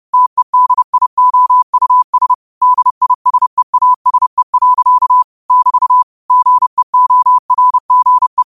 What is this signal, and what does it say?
Morse code, heard as a keyed tone, about 30 wpm, 1000 Hz. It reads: TEDIOUS DISEASE. <BT> GEORGE
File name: qotdfiles / 30wpm